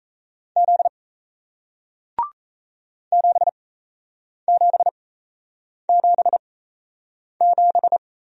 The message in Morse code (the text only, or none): Z 7777